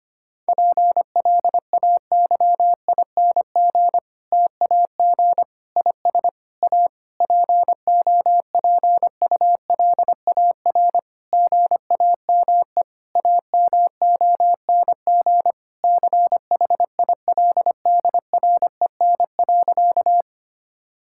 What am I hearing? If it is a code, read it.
Morse code, heard as a keyed tone, about 25 wpm, 700 Hz. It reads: PLAYING TAG IS A POPULAR GAME AMONG CHILDREN.
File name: practice / 25wpm / Difficult Sentences with Hints - 25wpm